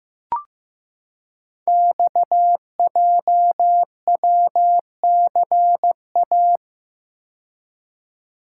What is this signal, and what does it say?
Morse code, heard as a keyed tone, about 15 wpm, 700 Hz. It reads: XJWCA
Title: Code Groups - 5 Letter:Numbers - 15wpm